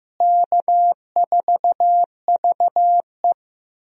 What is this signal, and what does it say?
Morse code, heard as a keyed tone, about 16 wpm, 700 Hz. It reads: K4VE
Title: USA Call Signs - 4 Characters - Repeated 2x - 15wpm